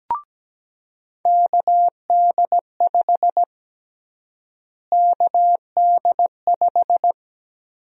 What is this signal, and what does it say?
Morse code, heard as a keyed tone, about 17 wpm, 700 Hz. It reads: KD5 KD5
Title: USA Call Sign Prefixes - Repeated 2x - 17wpm